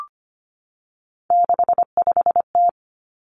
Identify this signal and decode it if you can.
Morse code, heard as a keyed tone, about 25 wpm, 700 Hz. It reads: 65T
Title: Three Challenging Letter-Number Combinations 25wpm